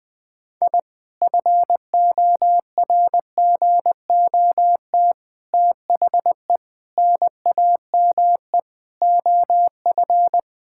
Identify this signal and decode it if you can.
Morse code, heard as a keyed tone, about 20 wpm, 700 Hz. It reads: I FORGOT THE NAME OF
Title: Sentences with 4-Word Phrases - 20wpm